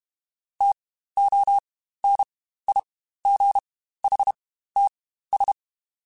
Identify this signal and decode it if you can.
Morse code, heard as a keyed tone, about 32 wpm, spaced as for 15 wpm, 800 Hz. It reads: TONIGHTS
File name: Spaced Characters / Spaced Characters-mp3s / Weather 4 14 sp @30